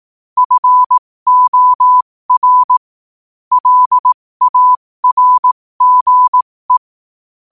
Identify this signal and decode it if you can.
Morse code, heard as a keyed tone, about 18 wpm, spaced as for 15 wpm, 1000 Hz. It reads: FOR LARGE
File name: qotdfiles / 15wpm